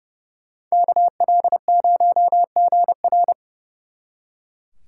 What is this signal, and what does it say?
Morse code, heard as a keyed tone, about 37 wpm, 700 Hz. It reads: KL0GR
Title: Speed Racing - USA 2x2 Call Signs 30wpm